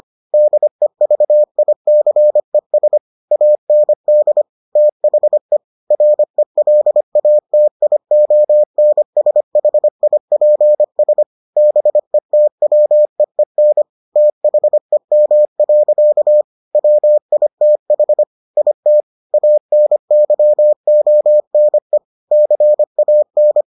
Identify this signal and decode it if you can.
Morse code, heard as a keyed tone, about 25 wpm, 600 Hz. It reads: DEVICES AND THE RELATIONSHIPS BETWEEN THEM. WITH IT ANYONE CAN